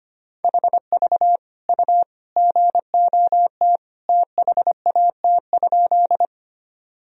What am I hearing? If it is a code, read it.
Morse code, heard as a keyed tone, about 25 wpm, 700 Hz. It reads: HV U GOT THAT?
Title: Sentences from Top 300 Words and Abbreviations 25wpm